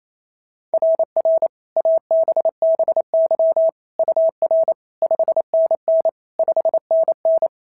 Morse code, heard as a keyed tone, about 28 wpm, 690 Hz. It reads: RR ABBY UR 5NN 5NN
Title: SOTA QSO Practice 28wpm